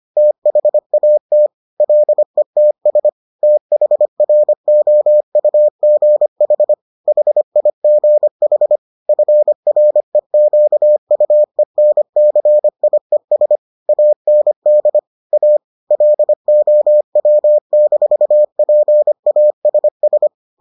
Morse code, heard as a keyed tone, about 25 wpm, 600 Hz. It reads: THAT LETS THROUGH HIGH FREQUENCIES AND A LOW-PASS